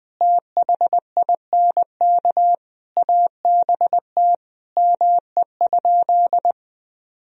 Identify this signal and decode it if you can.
Morse code, heard as a keyed tone, about 21 wpm, 700 Hz. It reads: THINK ABT ME?
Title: Sentences from Top 200 Words and Abbreviations 20wpm